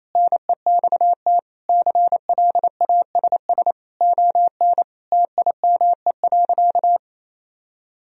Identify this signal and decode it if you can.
Morse code, heard as a keyed tone, about 28 wpm, 700 Hz. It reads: NEXT CLASS ON TIME.
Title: Sentences from Top 1000 Words - Review Part 2 - 28wpm